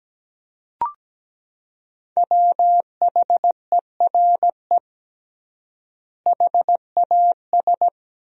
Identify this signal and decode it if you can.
Morse code, heard as a keyed tone, about 17 wpm, 700 Hz. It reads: WHERE HAS
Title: 3x Word-Spacing - Sentences from Top 300 Words - 17wpm